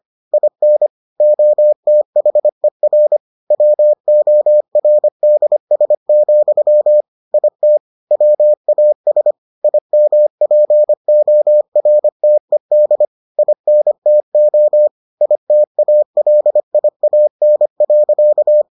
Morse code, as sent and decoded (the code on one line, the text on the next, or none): .. -. / --- - .... .-. / .-- --- .-. -.. ... --..-- / .. - / .-- .- ... / .. -- .--. --- .-. - . -.. / .. -. - --- / .. - .- .-.. .. .- -. .-.-.-
IN OTHER WORDS, IT WAS IMPORTED INTO ITALIAN.